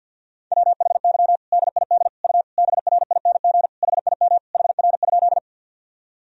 Morse code, heard as a keyed tone, about 46 wpm, 700 Hz. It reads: WHY DID U BRING HIM HR?
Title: Sentences from Top 500 Words and Abbreviations 50wpm